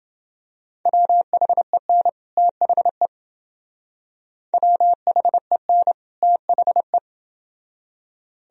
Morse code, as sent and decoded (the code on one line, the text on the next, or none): .-- .... -. / - .... / .-- .... -. / - .... .
WHEN THE WHEN THE